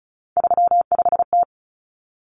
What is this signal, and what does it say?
Morse code, heard as a keyed tone, about 34 wpm, 700 Hz. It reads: 35T
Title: Common Cut Numbers - 3-Digit 35wpm